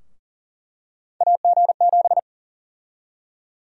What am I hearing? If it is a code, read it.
Morse code, heard as a keyed tone, about 40 wpm, 700 Hz. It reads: AG7
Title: USA Call Sign Prefixes - Repeated 2x - 40wpm